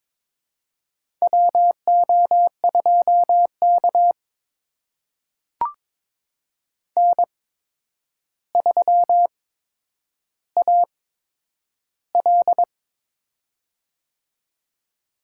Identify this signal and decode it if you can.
Morse code, heard as a keyed tone, about 23 wpm, spaced as for 9 wpm, 700 Hz. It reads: WO2K N3AL
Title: USA Call Signs - 4 Characters - 8x Character Spacing - 22wpm